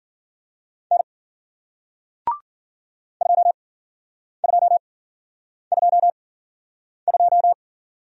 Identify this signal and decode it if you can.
Morse code, heard as a keyed tone, about 51 wpm, spaced as for 9 wpm, 700 Hz. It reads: N 2222